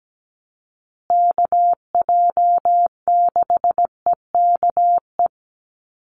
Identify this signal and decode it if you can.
Morse code, heard as a keyed tone, about 17 wpm, 700 Hz. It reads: KJ6EKE
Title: USA Call Signs Repeated 2x 17wpm